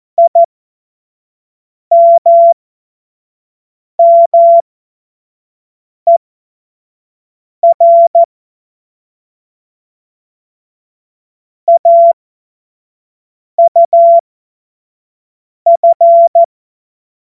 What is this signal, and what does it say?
Morse code, heard as a keyed tone, about 14 wpm, spaced as for 5 wpm, 680 Hz. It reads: IMMER AUF
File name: Morsekurs / Witze / Witz 18. April 2023 25 F70